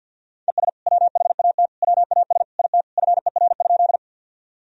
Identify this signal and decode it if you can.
Morse code, heard as a keyed tone, about 52 wpm, 700 Hz. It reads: ES WHAT WAS IT FER?
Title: Sentences from Top 100 Words and Abbreviations 50wpm